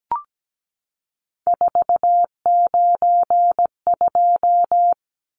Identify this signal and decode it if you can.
Morse code, heard as a keyed tone, about 17 wpm, 700 Hz. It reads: 492
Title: Numbers - 3-Digit 17wpm